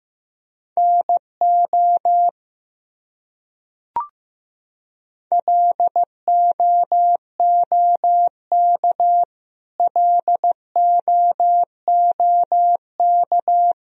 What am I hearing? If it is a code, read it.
Morse code, heard as a keyed tone, about 15 wpm, 700 Hz. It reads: NO LOOK LOOK
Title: Top 100 Words Repeated 15wpm